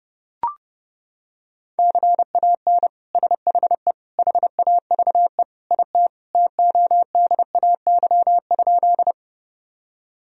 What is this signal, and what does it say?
Morse code, heard as a keyed tone, about 30 wpm, 700 Hz. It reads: CAN SHE HAVE IT TODAY?
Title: Sentences from Top 1000 Words - Review Part 4 - 30wpm